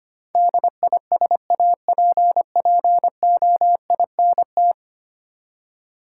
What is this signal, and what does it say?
Morse code, heard as a keyed tone, about 25 wpm, 700 Hz. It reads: DISAPPOINT
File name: Ten Letter Words - 25wpm